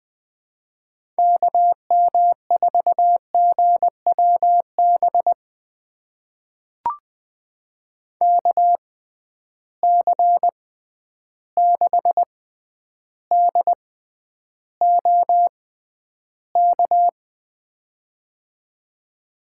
Morse code, heard as a keyed tone, about 20 wpm, spaced as for 10 wpm, 700 Hz. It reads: KM4GWB KC6DOK
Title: USA Call Signs - 6 Characters - 6x Character Spacing - 20wpm